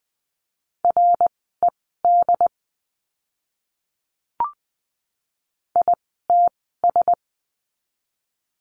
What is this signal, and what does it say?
Morse code, heard as a keyed tone, about 20 wpm, spaced as for 14 wpm, 700 Hz. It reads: RED ITS
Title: 101 Lesson 17f - Letter u - Words 20wpm